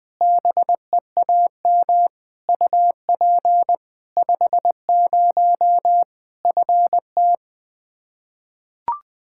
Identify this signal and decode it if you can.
Morse code, heard as a keyed tone, about 20 wpm, 700 Hz. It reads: BEAM UP 50 FT
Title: Practice QSO - Exchange 2 - Info - 20wpm